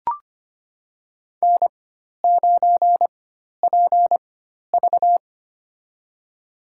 Morse code, heard as a keyed tone, about 25 wpm, spaced as for 12 wpm, 700 Hz. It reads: N9PV
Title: USA Call Signs - 4 Characters - 4x Character Spacing - 25wpm